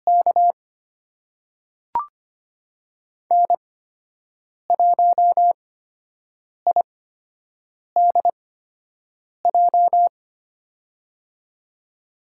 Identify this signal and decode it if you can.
Morse code, heard as a keyed tone, about 25 wpm, spaced as for 7 wpm, 700 Hz. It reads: K N1IDJ
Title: USA Call Signs - 5 Characters - 8x Character Spacing - 25wpm